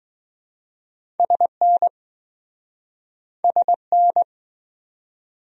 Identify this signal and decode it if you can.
Morse code, heard as a keyed tone, about 20 wpm, 700 Hz. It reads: SN SN